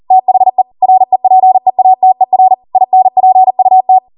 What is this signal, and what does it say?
Morse code, heard as a keyed tone, about 40 wpm, 750 Hz. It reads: THE REPEATER INPUT
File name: Morse / 40WPM